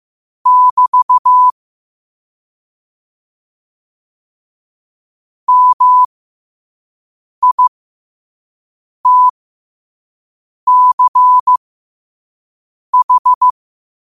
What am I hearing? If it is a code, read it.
Morse code, heard as a keyed tone, about 15 wpm, spaced as for 5 wpm, 1000 Hz. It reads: <BT> MITCH
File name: qotdfiles / 5wpm